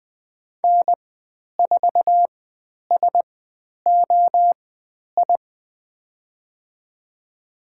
Morse code, heard as a keyed tone, about 20 wpm, spaced as for 10 wpm, 700 Hz. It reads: N4SOI
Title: 109 Lesson 18g - Letter m - Call Signs 20wpm